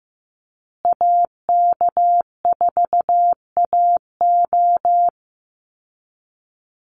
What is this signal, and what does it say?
Morse code, heard as a keyed tone, about 15 wpm, 700 Hz. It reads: AK4AO
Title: Call Signs Repeated 3x - 15wpm